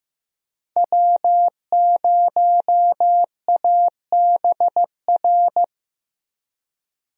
Morse code, heard as a keyed tone, about 15 wpm, 700 Hz. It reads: W0ABR
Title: USA Call Signs - 5 Characters - Repeated 3x - 15wpm